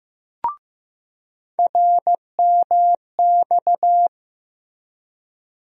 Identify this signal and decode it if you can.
Morse code, heard as a keyed tone, about 15 wpm, 700 Hz. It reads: RMX
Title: USA Call Sign Postfixes - 3 Letters 15wpm